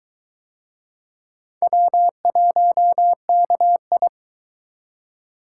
Morse code, heard as a keyed tone, about 20 wpm, 700 Hz. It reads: W1KI